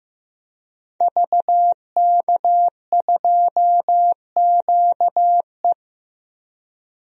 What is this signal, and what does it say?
Morse code, heard as a keyed tone, about 16 wpm, 700 Hz. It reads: VK2QE
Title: DXCC-100 Call Signs - 5 Characters - Repeated 3x - 15wpm